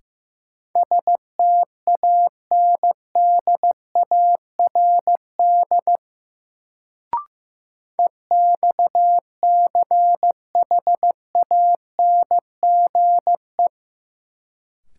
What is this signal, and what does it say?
Morse code, heard as a keyed tone, about 15 wpm, 700 Hz. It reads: STANDARD EXCHANGE
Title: Sentences from Top 1000 Words - Review Part 6 - New Word Drill - 15wpm